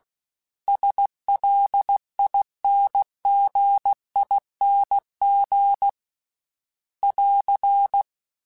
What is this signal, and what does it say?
Morse code, heard as a keyed tone, about 16 wpm, 800 Hz. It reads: SLINGING <AR>
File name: Special INGs-mp3s / Special INGs-mp3s / Dble Ings @15